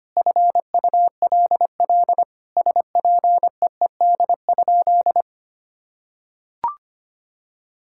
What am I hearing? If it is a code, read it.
Morse code, heard as a keyed tone, about 25 wpm, 700 Hz. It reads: FULL SPEED?
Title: Sentences from Top 1000 Words - Review Part 5 - 25wpm